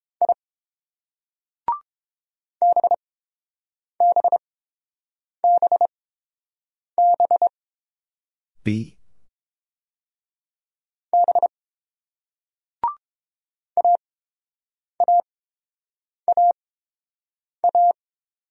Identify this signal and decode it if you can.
Morse code, heard as a keyed tone, about 27 wpm, spaced as for 7 wpm, 700 Hz. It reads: I BBBB B AAAA
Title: Speed-Racing Single Letters 22wpm